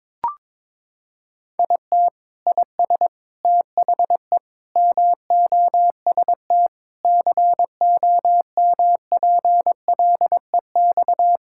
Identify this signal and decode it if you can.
Morse code, heard as a keyed tone, about 21 wpm, 700 Hz. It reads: IT IS THE MOST COMPLEX
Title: Sentences from Top 1000 Words - Review Part 6 - 22wpm